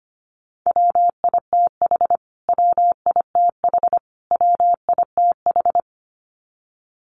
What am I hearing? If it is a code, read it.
Morse code, heard as a keyed tone, about 25 wpm, 700 Hz. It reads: WITH WITH WITH
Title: Top 200 Words Repeated 25wpm